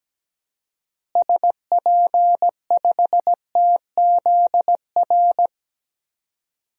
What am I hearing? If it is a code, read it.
Morse code, heard as a keyed tone, about 17 wpm, 700 Hz. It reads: SP5TZR
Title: DXCC-100 Call Signs - 6 Characters - Repeated 3x - 17wpm